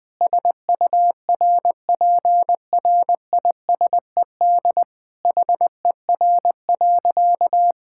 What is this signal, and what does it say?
Morse code, heard as a keyed tone, about 20 wpm, 700 Hz. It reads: SURPRISED HER.
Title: Sentences with 4-Word Phrases - 20wpm